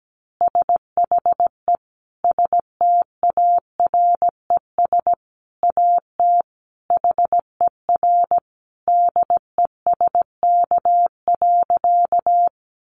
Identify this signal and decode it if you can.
Morse code, heard as a keyed tone, about 17 wpm, 700 Hz. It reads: SHE STARES AT HER DESK.
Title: Easy Phrases 17wpm